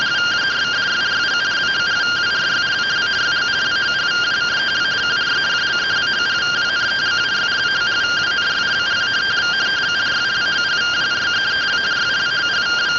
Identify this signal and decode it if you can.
Two-tone teleprinter signal, about 170 Hz shift.